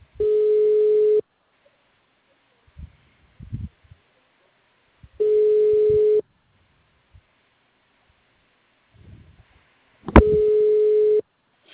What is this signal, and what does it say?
Ringback tone (Europe, 425 Hz).
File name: out- -2006- - - .106